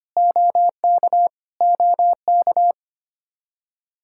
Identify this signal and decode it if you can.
Morse code, heard as a keyed tone, about 25 wpm, 700 Hz. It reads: OK OK
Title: Practice QSO - Exchange 1 - Beginning - 25wpm